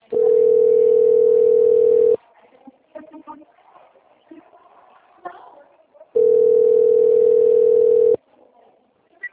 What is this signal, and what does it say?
Ringback tone (North America), 440 + 480 Hz, with speech.
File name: rg-1600-unknown- - - .41572